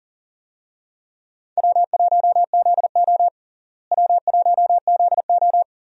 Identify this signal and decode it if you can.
Morse code, heard as a keyed tone, about 37 wpm, 700 Hz. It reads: W1ZO W1ZO
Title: USA Call Signs - 4 Characters - Repeated 2x - 40wpm